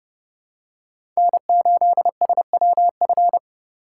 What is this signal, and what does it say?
Morse code, heard as a keyed tone, about 30 wpm, 700 Hz. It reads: N8SWF
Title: USA Call Signs - 5 Characters - 2x Character Spacing - 30wpm